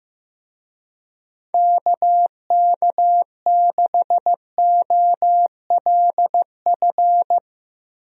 Morse code, heard as a keyed tone, about 15 wpm, 700 Hz. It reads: KK6OLF
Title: USA Call Signs - 6 Characters - Repeated 3x - 15wpm